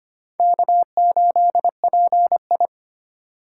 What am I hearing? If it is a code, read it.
Morse code, heard as a keyed tone, about 32 wpm, 700 Hz. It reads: K8PI